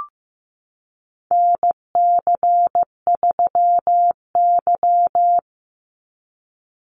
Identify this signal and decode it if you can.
Morse code, heard as a keyed tone, about 15 wpm, 700 Hz. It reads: NC3Y
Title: Call Signs Repeated 3x - 15wpm